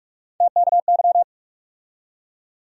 Morse code, heard as a keyed tone, about 41 wpm, 700 Hz. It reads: TKY
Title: USA Call Sign Postfixes - 3 Letters 45wpm